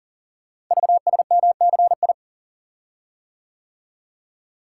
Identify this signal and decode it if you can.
Morse code, heard as a keyed tone, about 40 wpm, 700 Hz. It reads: VSMCI
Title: Code Groups - 5 Letter:Numbers - 40wpm